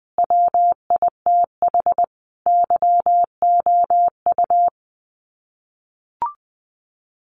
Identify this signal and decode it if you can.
Morse code, heard as a keyed tone, about 20 wpm, 700 Hz. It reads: WITH YOU
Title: Sets of 5 Words Encore2 20wpm